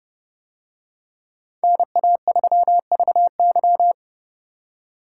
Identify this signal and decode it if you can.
Morse code, heard as a keyed tone, about 28 wpm, 700 Hz. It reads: NA3VY